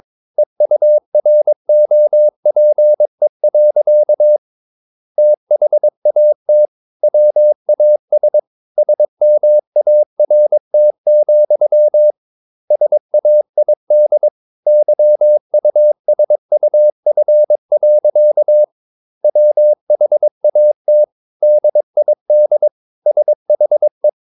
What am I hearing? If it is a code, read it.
Morse code, heard as a keyed tone, about 22 wpm, 600 Hz. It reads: EUROPE. THAT WAS SMART, SAID YUSUF. WHAT DID SHE
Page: Romanen Never af Ken Follet på engelsk.